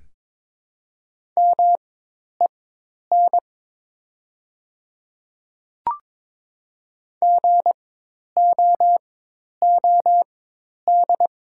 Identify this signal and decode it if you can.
Morse code, heard as a keyed tone, about 22 wpm, spaced as for 10 wpm, 700 Hz. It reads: MEN GOOD
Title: 4x Character Spacing - Top 200 Words - 22wpm